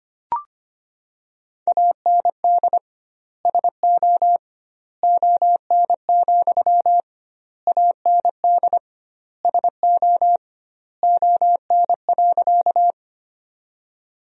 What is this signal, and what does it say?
Morse code, heard as a keyed tone, about 25 wpm, 700 Hz. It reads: AND SO ON, AND SO ON.
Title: 2x Word-Spacing - Sentences from Top 100 Words - 25wpm